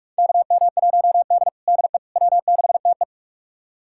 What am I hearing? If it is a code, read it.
Morse code, heard as a keyed tone, about 47 wpm, 700 Hz. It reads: KM1D DE W6TE
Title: USA Callsign DE Callsign - 4 Characters - 45wpm